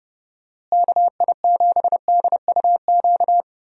Morse code, heard as a keyed tone, about 30 wpm, 700 Hz. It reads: KI7DUQ
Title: USA Call Signs - 6 Characters - Repeated 3x - 30wpm